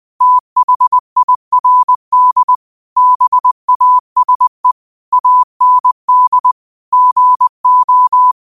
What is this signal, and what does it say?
Morse code, heard as a keyed tone, about 20 wpm, 1000 Hz. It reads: THIRD BASE AND GO
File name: qotdfiles / 20wpm